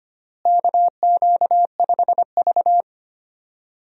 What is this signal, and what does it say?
Morse code, heard as a keyed tone, about 25 wpm, 700 Hz. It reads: KQ5V
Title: USA Call Signs - 4 Characters - Repeated 2x - 25wpm